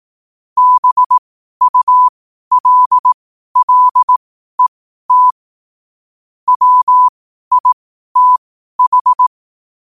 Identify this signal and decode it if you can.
Morse code, heard as a keyed tone, about 18 wpm, spaced as for 12 wpm, 1000 Hz. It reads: BULLET WITH
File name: qotdfiles / 12wpm